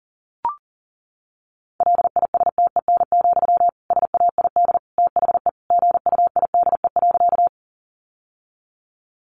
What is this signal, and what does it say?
Morse code, heard as a keyed tone, about 40 wpm, 700 Hz. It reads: LISTEN, SAID THE GUIDE.
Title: Sentences from Top 600 Words 40wpm